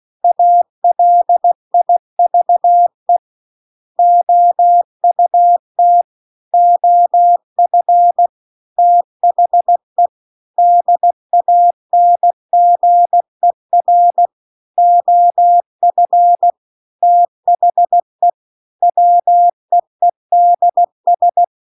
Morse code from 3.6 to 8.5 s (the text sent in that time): OUT OF